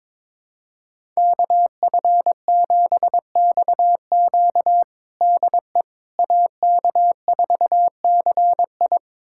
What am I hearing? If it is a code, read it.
Morse code, heard as a keyed tone, about 22 wpm, 700 Hz. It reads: KF7XQ DE AK4CI
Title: USA Callsign DE Callsign - 5 Characters - 22wpm